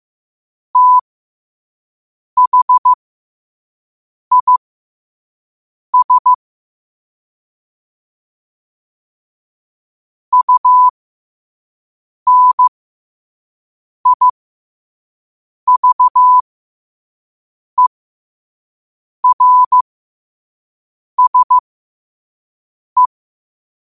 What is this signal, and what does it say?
Morse code, heard as a keyed tone, about 15 wpm, spaced as for 5 wpm, 1000 Hz. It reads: THIS UNIVERSE
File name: qotdfiles / 5wpm